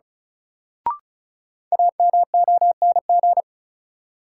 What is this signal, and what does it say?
Morse code, heard as a keyed tone, about 36 wpm, 700 Hz. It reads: AMONG
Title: Sentences from Top 500 Words - New Word Drill - 35wpm